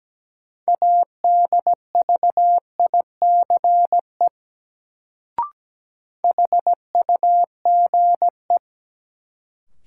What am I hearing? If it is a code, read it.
Morse code, heard as a keyed tone, about 17 wpm, 700 Hz. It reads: ADVICE HUGE
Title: Sentences from Top 1100 Words - New Word Drill Alt - 17wpm